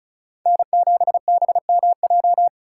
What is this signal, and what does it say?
Morse code, heard as a keyed tone, about 34 wpm, 700 Hz. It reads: N7BMJ
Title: USA Call Signs - 5 Characters - 4x Character Spacing - 35wpm